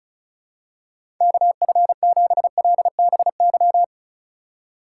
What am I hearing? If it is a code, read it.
Morse code, heard as a keyed tone, about 36 wpm, 700 Hz. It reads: KF7LBY
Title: USA Call Signs - 4x Character Spacing - 35wpm